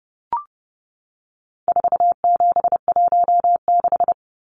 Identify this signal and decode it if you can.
Morse code, heard as a keyed tone, about 30 wpm, 700 Hz. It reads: 4716
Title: Numbers - 4-Digit 30wpm